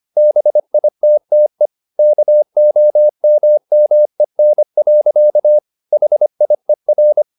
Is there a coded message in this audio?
Yes: Morse code from 0.2 to 7.2 s, heard as a keyed tone, about 25 wpm, 600 Hz. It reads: BITTE KOMMEN. HIER